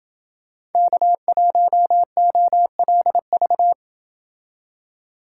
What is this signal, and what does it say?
Morse code, heard as a keyed tone, about 28 wpm, 700 Hz. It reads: K1OLV